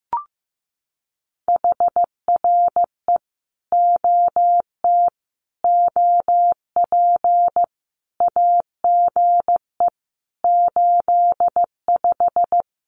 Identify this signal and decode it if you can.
Morse code, heard as a keyed tone, about 14 wpm, 700 Hz. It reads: HRE OT OP AGE 85
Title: Common QSO Phrases 15wpm